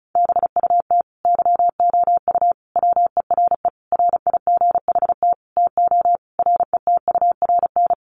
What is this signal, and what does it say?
Morse code, heard as a keyed tone, about 34 wpm, 700 Hz. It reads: BUT YOU WERE RIGHT TO RETURN
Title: Sentences from Top 800 Words 35wpm